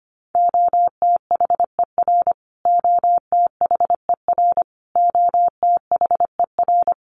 Morse code, heard as a keyed tone, about 25 wpm, 700 Hz. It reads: OTHER OTHER OTHER
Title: Top 500 Words Repeated 25wpm